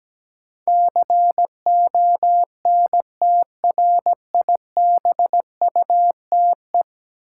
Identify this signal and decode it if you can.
Morse code, heard as a keyed tone, about 17 wpm, 700 Hz. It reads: CONTRIBUTE
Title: Sentences from Top 1400 Words - New Word Drill Alt - 17wpm